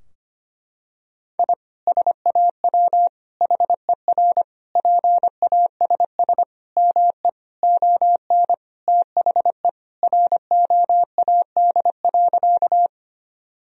Morse code, heard as a keyed tone, about 25 wpm, 700 Hz. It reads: I SAW HER PASS ME ON THE ROAD.
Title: Sentences from Top 1000 Words - Review Part 2 - 25wpm